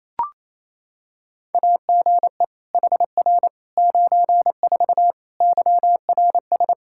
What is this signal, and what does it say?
Morse code, heard as a keyed tone, about 27 wpm, 700 Hz. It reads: AGE HR 94 YRS
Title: K1USN SST QSO Practice 28wpm